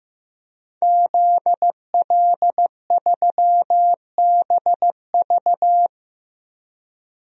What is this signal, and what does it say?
Morse code, heard as a keyed tone, about 15 wpm, 700 Hz. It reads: ZL3BV
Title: DXCC-100 Call Signs - 5 Characters - Repeated 3x - 15wpm